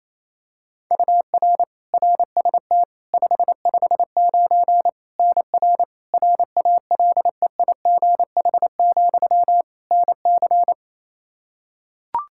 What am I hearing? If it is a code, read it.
Morse code, heard as a keyed tone, about 28 wpm, 700 Hz. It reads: UR RST 559 NR RALEIGH, NC
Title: Practice QSO - Exchange 1 - Beginning - 28wpm